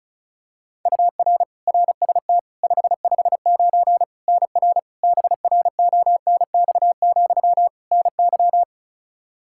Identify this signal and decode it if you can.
Morse code, heard as a keyed tone, about 36 wpm, 700 Hz. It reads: UR RST 559 NR BRONX, NY
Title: K1USN SST QSO Practice 35wpm